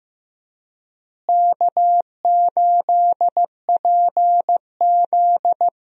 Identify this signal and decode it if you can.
Morse code, heard as a keyed tone, about 15 wpm, 700 Hz. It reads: K8PZ
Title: USA Call Signs - 4 Characters - Repeated 3x - 15wpm